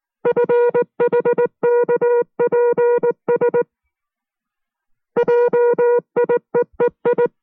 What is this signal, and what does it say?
Morse code, heard as a keyed tone, about 19 wpm, 480 Hz. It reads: FHKPS JIEEI